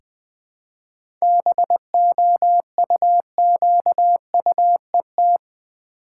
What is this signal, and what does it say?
Morse code, heard as a keyed tone, about 20 wpm, 700 Hz. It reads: BOUQUET
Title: Easy Words with Hints - 20wpm